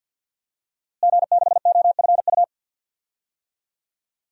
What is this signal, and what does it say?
Morse code, heard as a keyed tone, about 47 wpm, 700 Hz. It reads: G6OFU
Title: DXCC-100 Call Signs - 5 Characters - Repeated 2x - 50wpm